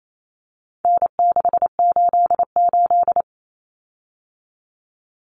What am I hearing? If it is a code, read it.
Morse code, heard as a keyed tone, about 28 wpm, 700 Hz. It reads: N688
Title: Common Cut Numbers - 4-Digit 28wpm